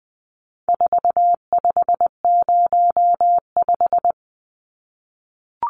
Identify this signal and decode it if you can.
Morse code, heard as a keyed tone, about 19 wpm, 700 Hz. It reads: 4505
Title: Numbers - 4-Digit 20wpm